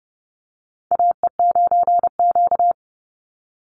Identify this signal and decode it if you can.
Morse code, heard as a keyed tone, about 30 wpm, 700 Hz. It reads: AE9Q
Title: NA SOTA Chaser Callsigns 30wpm